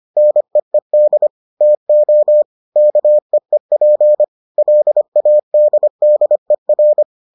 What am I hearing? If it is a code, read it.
Morse code, heard as a keyed tone, about 25 wpm, 600 Hz. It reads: NEED TO KEEP LADDER